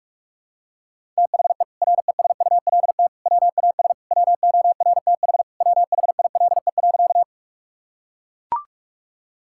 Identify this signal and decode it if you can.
Morse code, heard as a keyed tone, about 41 wpm, 700 Hz. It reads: THE RESULT WAS WORTH WHILE.
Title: Sentences from Top 1000 Words - Review Part 5 - 45wpm